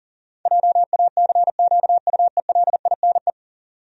Morse code, heard as a keyed tone, about 40 wpm, 700 Hz. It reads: JACQUELINE
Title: USA Common Names - 40wpm